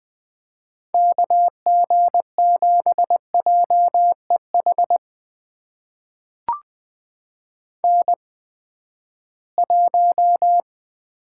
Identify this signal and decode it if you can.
Morse code, heard as a keyed tone, about 20 wpm, spaced as for 14 wpm, 700 Hz. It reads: KG7JEH N1
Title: USA Call Signs - 8x Character Spacing - 20wpm